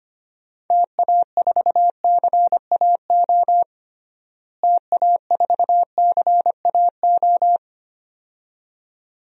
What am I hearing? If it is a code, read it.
Morse code, heard as a keyed tone, about 25 wpm, 700 Hz. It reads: TA4CAO TA4CAO
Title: DXCC-100 Call Signs - 6 Characters - Repeated 2x - 25wpm